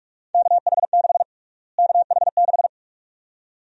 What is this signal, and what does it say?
Morse code, heard as a keyed tone, about 48 wpm, 700 Hz. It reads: KH6 KH6
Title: USA Call Sign Prefixes - Repeated 2x - 45wpm